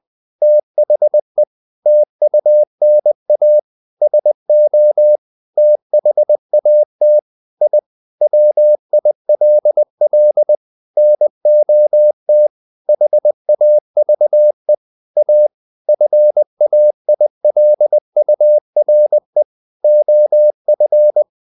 Morse code, heard as a keyed tone, about 20 wpm, 600 Hz. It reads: THE TUNA SO THAT I WILL NOT HAVE A FAILURE OF
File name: the old man and the sea / the old man and the sea part 1 20wpm